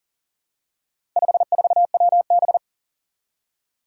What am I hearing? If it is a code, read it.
Morse code, heard as a keyed tone, about 40 wpm, 700 Hz. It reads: 54WB
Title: Four Challenging Letter-Number Combinations 40wpm